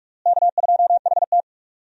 Morse code, heard as a keyed tone, about 46 wpm, 700 Hz. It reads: K2HT
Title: USA Call Signs - 4 Characters - Repeated 3x - 45wpm